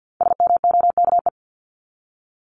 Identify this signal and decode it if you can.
Morse code, heard as a keyed tone, about 48 wpm, 700 Hz. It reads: SMOKE
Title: Top 1501-1750 Words 50wpm